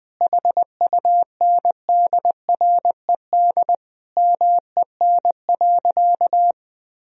Morse code, heard as a keyed tone, about 20 wpm, 700 Hz. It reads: HUNDRED MEN.
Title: Sentences from Top 1000 Words - Review Part 2 - 20wpm